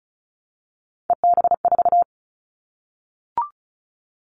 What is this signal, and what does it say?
Morse code, heard as a keyed tone, about 34 wpm, 700 Hz. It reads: EB4
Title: Callsign Prefixes for Top 25 DXCC - 35wpm